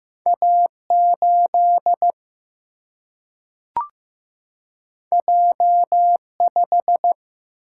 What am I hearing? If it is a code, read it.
Morse code, heard as a keyed tone, about 15 wpm, 700 Hz. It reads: A8 J5
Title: Single-Letter Single-Number 15wpm